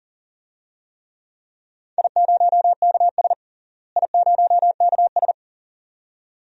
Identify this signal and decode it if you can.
Morse code, heard as a keyed tone, about 40 wpm, 700 Hz. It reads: I0KS I0KS
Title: DXCC-100 Call Signs - 4 Characters - Repeated 2x - 40wpm